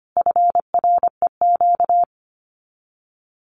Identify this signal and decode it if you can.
Morse code, heard as a keyed tone, about 25 wpm, 700 Hz. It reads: FREQ
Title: Abbreviations Repeated 3x - 25wpm